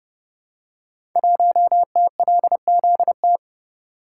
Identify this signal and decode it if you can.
Morse code, heard as a keyed tone, about 30 wpm, 700 Hz. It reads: 1TLZT
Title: Code Groups - 5 Letter:Numbers - 30wpm